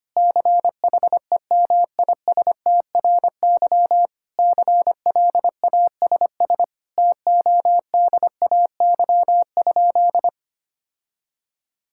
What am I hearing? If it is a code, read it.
Morse code, heard as a keyed tone, about 25 wpm, 700 Hz. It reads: CHEMISTRY CLASS TODAY?
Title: Sentences from Top 1000 Words - Review Part 6 - 25wpm